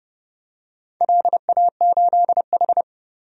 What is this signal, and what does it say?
Morse code, heard as a keyed tone, about 30 wpm, 700 Hz. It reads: LA8H
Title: DXCC-100 Call Signs - 4 Characters - Repeated 2x - 30wpm